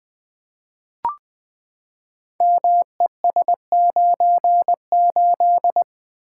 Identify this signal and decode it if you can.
Morse code, heard as a keyed tone, about 20 wpm, 700 Hz. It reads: MES98
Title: Code Groups - 5 Letter:Numbers - 20wpm